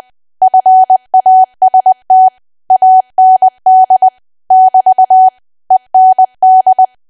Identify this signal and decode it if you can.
Morse code, heard as a keyed tone, about 20 wpm, 750 Hz. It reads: FAST AND <BT> END
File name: Archive / 20 WPM / 20WPM